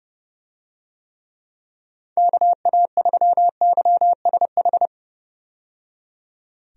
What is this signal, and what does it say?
Morse code, heard as a keyed tone, about 31 wpm, 700 Hz. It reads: KA3YSH
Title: USA Call Signs - 6 Characters - 30wpm